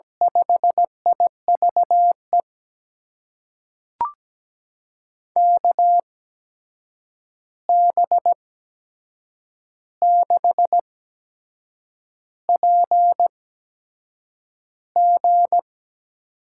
Morse code, heard as a keyed tone, about 17 wpm, spaced as for 7 wpm, 700 Hz. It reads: 5IVE KB6PG
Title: USA Call Signs - 6 Characters - 8x Character Spacing - 17wpm